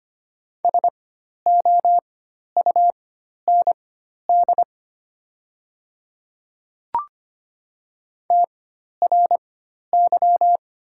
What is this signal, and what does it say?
Morse code, heard as a keyed tone, about 25 wpm, spaced as for 12 wpm, 700 Hz. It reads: SOUND TRY
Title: 4x Character Spacing - Top 200 Words - 25wpm